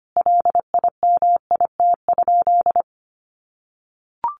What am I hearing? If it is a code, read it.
Morse code, heard as a keyed tone, about 25 wpm, 700 Hz. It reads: LIMIT?
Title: Sentences from Top 1500 Words 25wpm